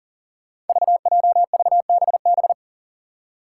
Morse code, heard as a keyed tone, about 40 wpm, 700 Hz. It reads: VJVBB
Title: Five Challenging Letter-Number Combinations 40wpm